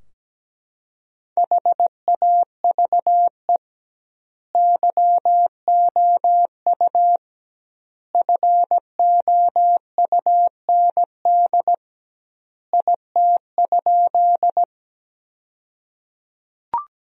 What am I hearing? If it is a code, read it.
Morse code, heard as a keyed tone, about 17 wpm, 700 Hz. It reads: HAVE YOU FOUND IT?
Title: 2x Word-Spacing - Sentences from Top 300 Words - 17wpm